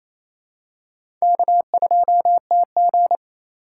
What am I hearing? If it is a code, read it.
Morse code, heard as a keyed tone, about 27 wpm, 700 Hz. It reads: K2TG
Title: USA Call Signs - 4 Characters - 2x Character Spacing - 28wpm